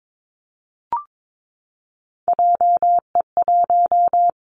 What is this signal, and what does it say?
Morse code, heard as a keyed tone, about 22 wpm, 700 Hz. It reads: JE1
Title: Callsign Prefixes for Top 25 DXCC - 22wpm